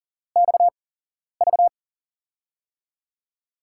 Morse code, heard as a keyed tone, about 40 wpm, spaced as for 11 wpm, 700 Hz. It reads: XV
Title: USA Call Signs - 8x Character Spacing - 40wpm